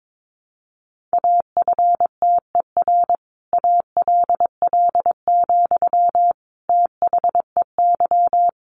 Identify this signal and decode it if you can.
Morse code, heard as a keyed tone, about 22 wpm, 700 Hz. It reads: AFTER ALL, THEY
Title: Sentences from Top 1100 Words 22wpm